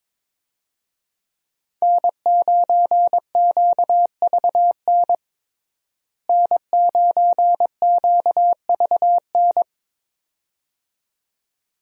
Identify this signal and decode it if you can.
Morse code, heard as a keyed tone, about 23 wpm, 700 Hz. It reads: N9QVN N9QVN
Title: USA Call Signs - 5 Characters - Repeated 2x - 22wpm